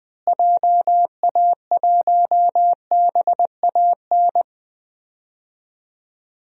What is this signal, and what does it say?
Morse code, heard as a keyed tone, about 20 wpm, 700 Hz. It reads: JA1BAN
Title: DXCC-100 Call Signs - 6 Characters - Repeated 2x - 20wpm